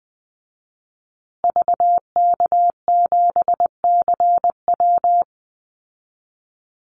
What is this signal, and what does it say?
Morse code, heard as a keyed tone, about 19 wpm, 700 Hz. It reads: VK7CW
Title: Call Signs Repeated 2x - 20wpm